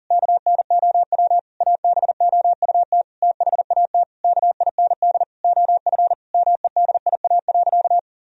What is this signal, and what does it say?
Morse code, heard as a keyed tone, about 40 wpm, 700 Hz. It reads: KNOW ABOUT THAT KIND OF MEDIA.
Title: Sentences from Top 1000 Words - Review Part 4 - 40wpm